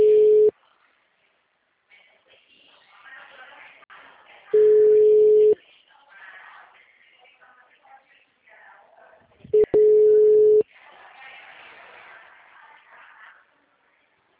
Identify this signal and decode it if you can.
Ringback tone (Europe, 425 Hz), with speech.